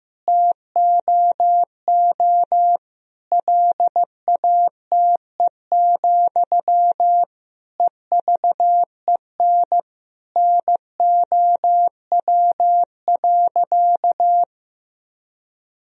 Morse code, heard as a keyed tone, about 16 wpm, 700 Hz. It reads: TOO LATE, EVEN NOW.
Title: Sentences from Top 1000 Words - Review Part 1 - 15wpm